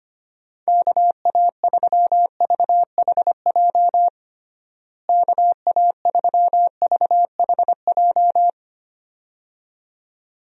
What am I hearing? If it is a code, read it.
Morse code, heard as a keyed tone, about 25 wpm, 700 Hz. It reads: KA3VHJ KA3VHJ
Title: USA Call Signs - 6 Characters - Repeated 2x - 25wpm